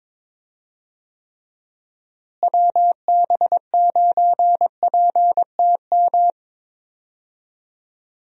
Morse code, heard as a keyed tone, about 22 wpm, 700 Hz. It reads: WB9PTM
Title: USA Call Signs - 6 Characters - Repeated 3x - 22wpm